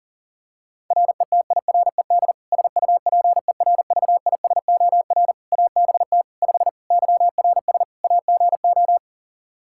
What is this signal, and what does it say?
Morse code, heard as a keyed tone, about 40 wpm, 700 Hz. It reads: RETIRED SUPERVISOR ABT 5 YRS AGO
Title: Practice QSO - Exchange 3 - More Info - 40wpm